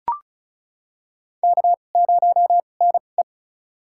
Morse code, heard as a keyed tone, about 34 wpm, spaced as for 25 wpm, 700 Hz. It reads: K0NE
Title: USA Call Signs - 2x Character Spacing - 35wpm